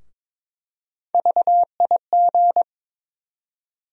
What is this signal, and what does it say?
Morse code, heard as a keyed tone, about 22 wpm, 700 Hz. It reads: VIG
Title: USA Call Sign Postfixes - 3 Letters 22wpm